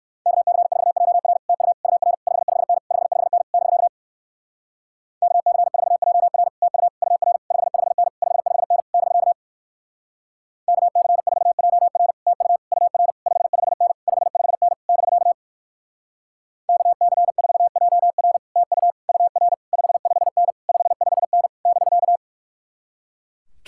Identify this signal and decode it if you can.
Morse code, heard as a keyed tone, about 67 wpm, 700 Hz. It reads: KC4JR TU UR 55N 55N <BK> KC4JR TU UR 55N 55N <BK> KC4JR TU UR 55N 55N <BK> KC4JR TU UR 55N 55N <BK>